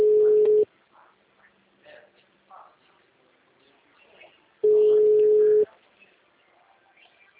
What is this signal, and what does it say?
Ringback tone (Europe, 425 Hz), with speech.